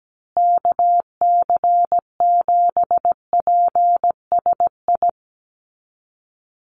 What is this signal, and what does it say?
Morse code, heard as a keyed tone, about 17 wpm, 700 Hz. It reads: KC7PSI
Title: USA Call Signs Repeated 3x 17wpm